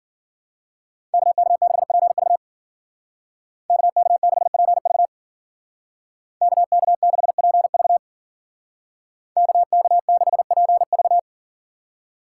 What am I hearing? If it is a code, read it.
Morse code, heard as a keyed tone, about 52 wpm, 700 Hz. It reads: KK6PV KK6PV KK6PV KK6PV